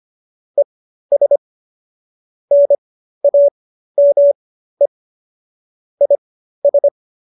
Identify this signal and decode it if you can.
Morse code, heard as a keyed tone, about 25 wpm, spaced as for 13 wpm, 570 Hz. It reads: ES NAME IS
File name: qso215b 13